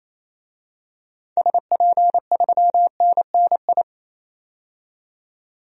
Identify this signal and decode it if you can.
Morse code, heard as a keyed tone, about 28 wpm, 700 Hz. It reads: SP3NNI
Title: DXCC-100 Call Signs - 6 Characters - 28wpm